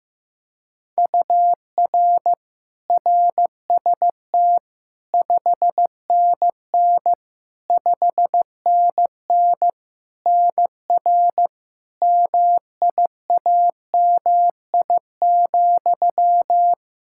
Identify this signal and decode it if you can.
Morse code, heard as a keyed tone, about 15 wpm, 700 Hz. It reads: UR RST 5NN 5NN NR MIAMI,
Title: K1USN SST QSO Practice 15wpm